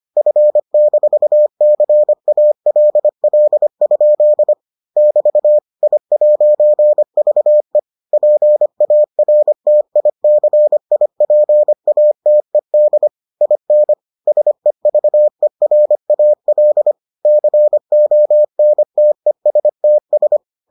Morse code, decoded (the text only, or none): F-CALL? <BT> I'VE PARTICIPATED IN SEVERAL CONTESTS